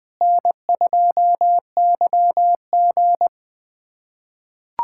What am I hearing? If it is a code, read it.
Morse code, heard as a keyed tone, about 20 wpm, 700 Hz. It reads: N2YG
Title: K1USN SST QSO Practice - 20wpm